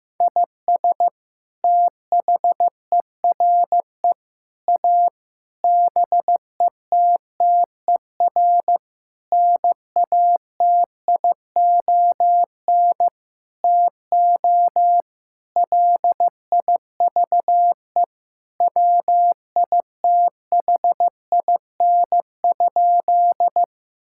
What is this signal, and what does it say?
Morse code, heard as a keyed tone, about 16 wpm, 700 Hz. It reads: IS THERE A BETTER NATION TO LIVE WITHIN?
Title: Sentences from Top 1000 Words - Review Part 6 - 15wpm